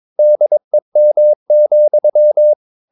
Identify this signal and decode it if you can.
Morse code, heard as a keyed tone, about 22 wpm, 600 Hz. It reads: DEM,
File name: Ferie QRS 22wpm